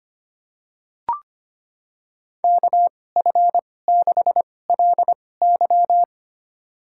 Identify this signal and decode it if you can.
Morse code, heard as a keyed tone, about 25 wpm, spaced as for 18 wpm, 700 Hz. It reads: KF6LY
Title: USA Call Signs - 5 Characters - 2x Character Spacing - 25wpm